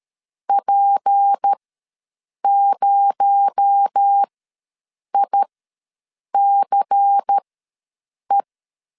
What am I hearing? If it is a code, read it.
Morse code, heard as a keyed tone, about 13 wpm, spaced as for 7 wpm, 800 Hz. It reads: P0ICE
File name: LEZIONI / 6 PKX0 / 6 PKX0 ws 12wpm 3448